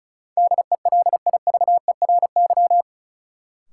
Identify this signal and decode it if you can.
Morse code, heard as a keyed tone, about 36 wpm, 700 Hz. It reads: DELIVERY
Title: Sentences from Top 1500 Words - New Word Drill - 35wpm